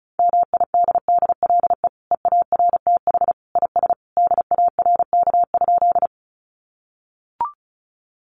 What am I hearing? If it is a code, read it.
Morse code, heard as a keyed tone, about 34 wpm, 700 Hz. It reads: MIDDLE EARTH IS DARK?
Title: Sentences from Top 900 Words 35wpm